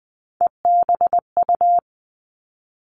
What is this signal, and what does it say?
Morse code, heard as a keyed tone, about 20 wpm, 700 Hz. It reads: EBU
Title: All Cut Numbers - 3-Digit 20wpm